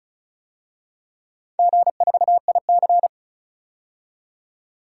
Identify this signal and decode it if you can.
Morse code, heard as a keyed tone, about 36 wpm, 700 Hz. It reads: G4IC
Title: DXCC-100 Call Signs - 4 Characters - 35wpm